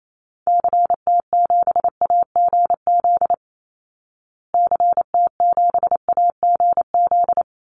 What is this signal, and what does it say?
Morse code, heard as a keyed tone, about 28 wpm, 700 Hz. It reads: CT7AGZ CT7AGZ
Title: Call Signs Repeated 2x - 28wpm